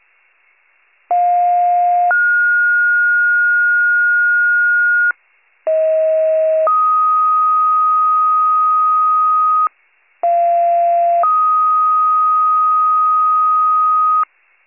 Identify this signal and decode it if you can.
Two-tone page, heard as a one-second tone, then a three-second tone, about 710 then 1430 Hz, about 640 then 1250 Hz, about 690 then 1250 Hz. It is heard from 1.1 to 14.2 s.